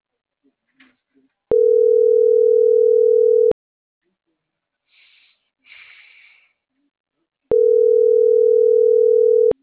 Ringback tone (North America), 440 + 480 Hz.